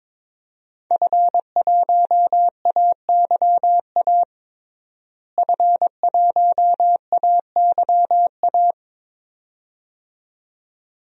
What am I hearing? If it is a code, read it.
Morse code, heard as a keyed tone, about 21 wpm, 700 Hz. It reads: F1AYA F1AYA
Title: DXCC-100 Call Signs - 5 Characters - Repeated 2x - 22wpm